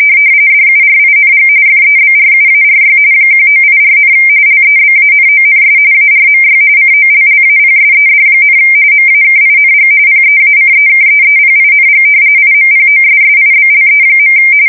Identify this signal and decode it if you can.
RTTY, about 45.45 baud. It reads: DURING FEBRUARY 1984, LEBANONS ARMY COLLAPSED, MAKING PEACEKEEPING IMPOSSIBLE.
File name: fsk170 / battleshipnj